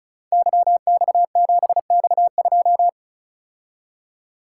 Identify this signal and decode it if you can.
Morse code, heard as a keyed tone, about 34 wpm, 700 Hz. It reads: YX7X2
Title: Code Groups - 5 Letter:Numbers - 35wpm